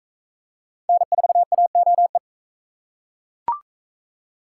Morse code, heard as a keyed tone, about 41 wpm, 700 Hz. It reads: N4AOE